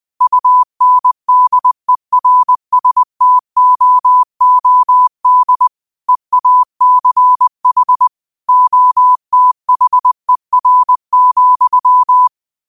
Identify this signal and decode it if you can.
Morse code, heard as a keyed tone, about 20 wpm, 1000 Hz. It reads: UNDERSTOOD EACH OTHER,